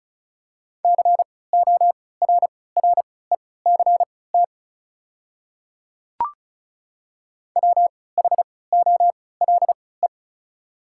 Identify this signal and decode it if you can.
Morse code, heard as a keyed tone, about 34 wpm, spaced as for 20 wpm, 690 Hz. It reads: CORRECT WHOLE
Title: 3x Character Spacing - Top 500 Words - 35wpm